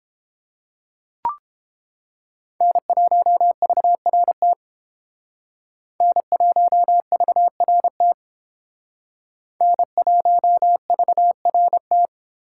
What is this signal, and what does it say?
Morse code, heard as a keyed tone, about 30 wpm, 700 Hz. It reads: N1VRT N1VRT N1VRT